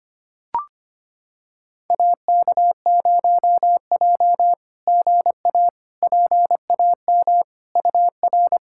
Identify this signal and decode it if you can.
Morse code, heard as a keyed tone, about 25 wpm, 700 Hz. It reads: AK0J GA PAM UR
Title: SOTA QSO Practice 25wpm